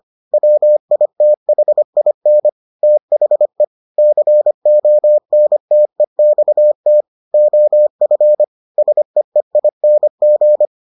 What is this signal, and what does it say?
Morse code, heard as a keyed tone, about 25 wpm, 600 Hz. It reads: WITHIN THE CONTEXT OF SEEING